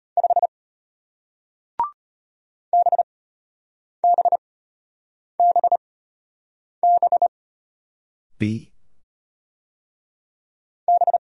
Morse code, heard as a keyed tone, about 34 wpm, 700 Hz. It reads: 5 B B B B B